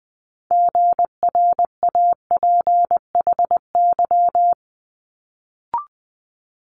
Morse code, heard as a keyed tone, about 20 wpm, 700 Hz. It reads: GRAPHY
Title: Multi-letter Phonemes Quick Fire 20wpm